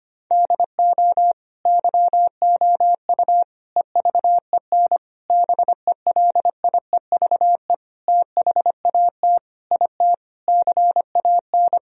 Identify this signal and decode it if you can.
Morse code, heard as a keyed tone, about 25 wpm, 700 Hz. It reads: DO YOU EVEN BELIEVE THAT IT CAN
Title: Sentences from Top 1000 Words - Review Part 1 - 25wpm